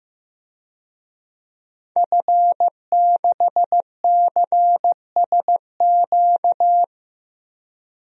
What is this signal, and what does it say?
Morse code, heard as a keyed tone, about 15 wpm, 700 Hz. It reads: F6CSQ
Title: DXCC-100 Call Signs - 5 Characters - 15wpm